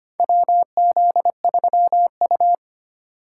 Morse code, heard as a keyed tone, about 25 wpm, 700 Hz. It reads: WZ3U
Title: USA Call Signs - 4 Characters - 3x Character Spacing - 25wpm